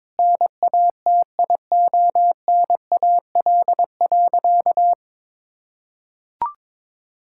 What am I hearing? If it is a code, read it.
Morse code, heard as a keyed tone, about 22 wpm, 700 Hz. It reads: NATIONAL.
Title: Sentences from Top 1000 Words - Review Part 5 - 22wpm